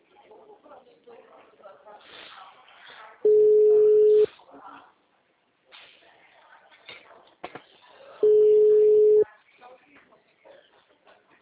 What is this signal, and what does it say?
Ringback tone (Europe, 425 Hz), with speech.
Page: Limeira - Telemarket